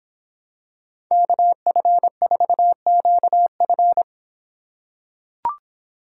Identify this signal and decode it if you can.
Morse code, heard as a keyed tone, about 25 wpm, 700 Hz. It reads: KF4QF